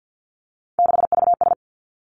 Morse code, heard as a keyed tone, about 50 wpm, 700 Hz. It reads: 6VS
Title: Three Challenging Letter-Number Combinations 50wpm